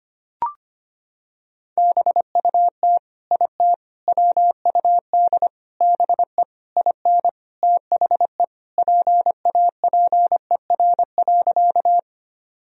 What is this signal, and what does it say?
Morse code, heard as a keyed tone, about 25 wpm, 700 Hz. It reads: BUT IT WUD BE IN THE PAPER.
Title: Sentences from Top 300 Words and Abbreviations 25wpm